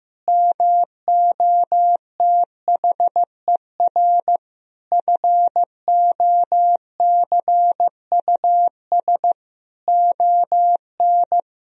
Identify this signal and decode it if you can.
Morse code, heard as a keyed tone, about 15 wpm, 700 Hz. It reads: MOTHER FOCUS ON
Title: Sentences from Top 1000 Words - Review Part 4 - 15wpm